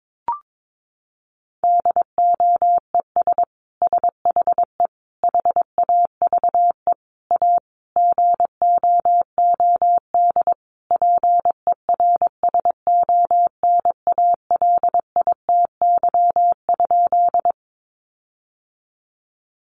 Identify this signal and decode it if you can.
Morse code, heard as a keyed tone, about 22 wpm, 700 Hz. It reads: DOES SHE HAVE A GOOD PERSONALITY?
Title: Sentences from Top 1400 Words 22wpm